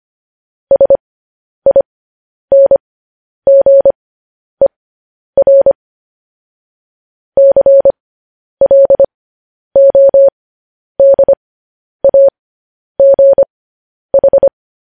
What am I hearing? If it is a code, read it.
Morse code, heard as a keyed tone, about 25 wpm, spaced as for 10 wpm, 550 Hz. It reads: SINGER CLODAGH
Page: news headlines in morse code at 10 wpm